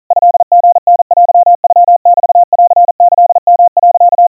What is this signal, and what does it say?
Morse code (Japanese Wabun code), heard as a keyed tone, about 40 wpm, 700 Hz. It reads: トリタテノメンキヨ、
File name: kawa / Morse sample / kamyu 150 cpm 700 Hz Crispy